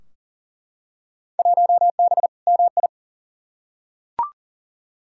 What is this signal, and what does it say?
Morse code, heard as a keyed tone, about 40 wpm, 700 Hz. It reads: 1B MI
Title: Field Day Exchanges - 40wpm